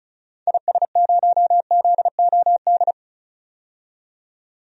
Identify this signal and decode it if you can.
Morse code, heard as a keyed tone, about 36 wpm, 700 Hz. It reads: IS0ZOD
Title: DXCC-100 Call Signs - 6 Characters - 35wpm